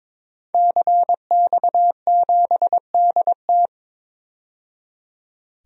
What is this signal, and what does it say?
Morse code, heard as a keyed tone, about 23 wpm, 700 Hz. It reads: CX7DT
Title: DXCC-100 Call Signs - 5 Characters - 22wpm